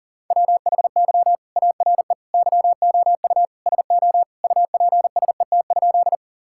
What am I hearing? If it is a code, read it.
Morse code, heard as a keyed tone, about 40 wpm, 700 Hz. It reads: WHY ARE YOU SO UPSET?
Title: Sentences from Top 1000 Words - Review Part 5 - 40wpm